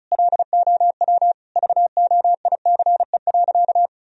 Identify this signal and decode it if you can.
Morse code, heard as a keyed tone, about 34 wpm, 700 Hz. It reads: LOW VOICE.
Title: Sentences from Top 1000 Words - Review Part 1 - 35wpm